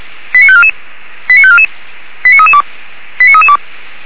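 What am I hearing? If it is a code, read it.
Five-tone selective call (ZVEI), heard as quick runs of five tones, tones of about 70 ms. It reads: 89544 89544 89332 89332
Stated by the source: Five-tone ZVEI selective call received on SDR, fire brigade Feuerwehr Orlamünde.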